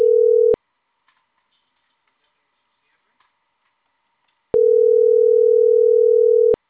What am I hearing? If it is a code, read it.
Ringback tone (North America), 440 + 480 Hz, with speech.
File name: CENTRAL CAEA / out agcaea10 - .1487